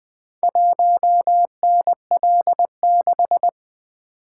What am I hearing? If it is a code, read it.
Morse code, heard as a keyed tone, about 20 wpm, 700 Hz. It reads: 1NL6
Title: Four Letter-Number Combinations - 20wpm